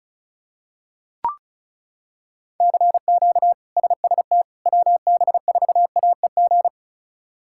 Morse code, heard as a keyed tone, about 34 wpm, 700 Hz. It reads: CQ SST WB4AEG
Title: K1USN SST QSO Practice - 35wpm